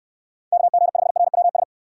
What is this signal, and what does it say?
Morse code, heard as a keyed tone, about 69 wpm, 700 Hz. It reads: KG5RWS